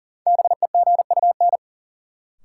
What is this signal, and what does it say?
Morse code, heard as a keyed tone, about 40 wpm, 700 Hz. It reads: BEGUN
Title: Sentences from Top 1000 Words - Review Part 6 - New Word Drill - 40wpm